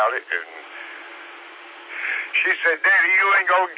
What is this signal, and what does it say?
Emergency Alert System (SAME) tones, heard as harsh buzzing bursts, with speech. It is heard from 0.8 to 3.4 s.